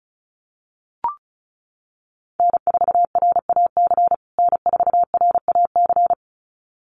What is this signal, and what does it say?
Morse code, heard as a keyed tone, about 34 wpm, 700 Hz. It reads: N4RAC N4RAC
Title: USA Call Signs Repeated 2x 35wpm